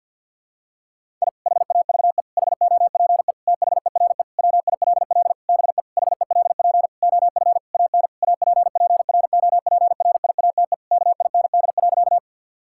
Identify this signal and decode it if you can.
Morse code, heard as a keyed tone, about 52 wpm, 700 Hz. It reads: I HAVE HOPE THERE WILL BE HELP OF AN APPROPRIATE KIND.
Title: Sentences from Top 1000 Words - Review Part 5 - 50wpm